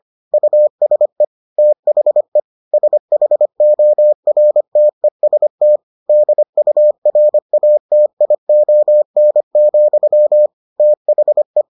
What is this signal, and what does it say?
Morse code, heard as a keyed tone, about 25 wpm, 600 Hz. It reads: USE THE SHORTEST DURATION, THE